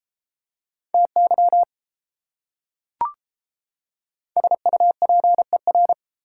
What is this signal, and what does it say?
Morse code, heard as a keyed tone, about 32 wpm, 700 Hz. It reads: TY SUPER